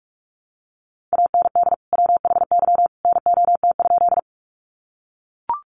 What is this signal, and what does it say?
Morse code, heard as a keyed tone, about 44 wpm, 700 Hz. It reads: AND WHY NOT?
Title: Sentences from Top 600 Words 45wpm